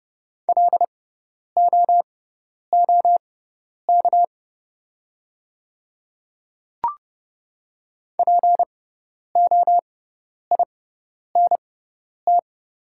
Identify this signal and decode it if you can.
Morse code, heard as a keyed tone, about 28 wpm, spaced as for 10 wpm, 690 Hz. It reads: LOOK POINT
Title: 6x Character Spacing - Top 200 Words - 30wpm